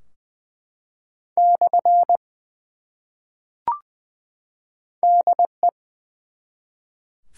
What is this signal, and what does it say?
Morse code, heard as a keyed tone, about 20 wpm, 700 Hz. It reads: / DE
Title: Top 180 QSO Elements 20wpm